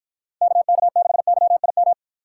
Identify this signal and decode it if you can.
Morse code, heard as a keyed tone, about 48 wpm, 700 Hz. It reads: KK6YIM